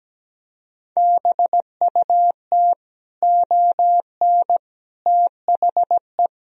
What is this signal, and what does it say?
Morse code, heard as a keyed tone, about 17 wpm, 700 Hz. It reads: BUT ON THE
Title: Speed-Racing - Sets of 5 Words 17wpm